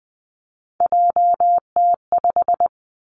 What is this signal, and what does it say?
Morse code, heard as a keyed tone, about 20 wpm, 700 Hz. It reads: JT5
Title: Three Challenging Letter-Number Combinations 20wpm